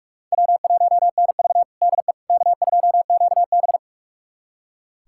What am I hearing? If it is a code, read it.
Morse code, heard as a keyed tone, about 41 wpm, 700 Hz. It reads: W1NV DE K2QB
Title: USA Callsign DE Callsign - 4 Characters - 45wpm